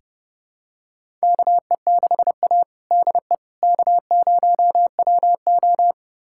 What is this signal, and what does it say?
Morse code, heard as a keyed tone, about 30 wpm, 700 Hz. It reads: KE6A DE K0WO
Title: USA Callsign DE Callsign - 4 Characters - 30wpm